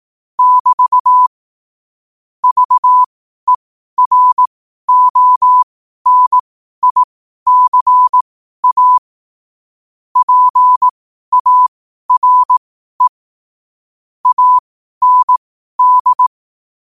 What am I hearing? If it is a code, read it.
Morse code, heard as a keyed tone, about 18 wpm, spaced as for 12 wpm, 1000 Hz. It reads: <BT> VERONICA PARE AND
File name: qotdfiles / 12wpm